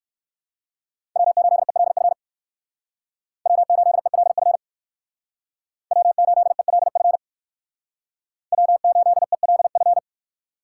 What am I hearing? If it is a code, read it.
Morse code, heard as a keyed tone, about 59 wpm, 700 Hz. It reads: W8ELF W8ELF W8ELF W8ELF